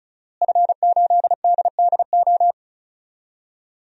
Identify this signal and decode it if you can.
Morse code, heard as a keyed tone, about 36 wpm, 700 Hz. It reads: F8DDO
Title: DXCC-100 Call Signs - 5 Characters - Repeated 3x - 35wpm